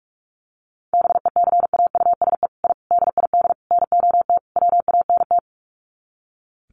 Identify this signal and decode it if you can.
Morse code, heard as a keyed tone, about 44 wpm, 700 Hz. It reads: BECAUSE I DID NOT WANT
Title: Sets of 5 Words Encore2 45wpm